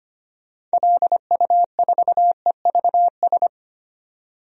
Morse code, heard as a keyed tone, about 25 wpm, 700 Hz. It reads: LU4EVS
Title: DXCC-100 Call Signs - 6 Characters - 25wpm